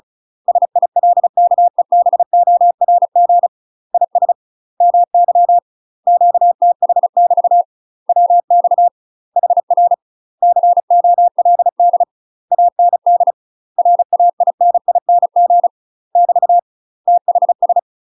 Morse code, heard as a keyed tone, about 36 wpm, 700 Hz. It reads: SILKEBORG IS MY QTH <BT> WX HR COLD AND RAINING <BT> THS
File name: UK MASTER QSOs TRAINING / 35WPM / UK MASTER QSOs 0034